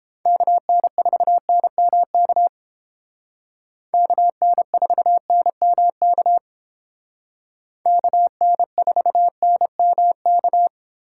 Morse code, heard as a keyed tone, about 30 wpm, 700 Hz. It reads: KN4NMK KN4NMK KN4NMK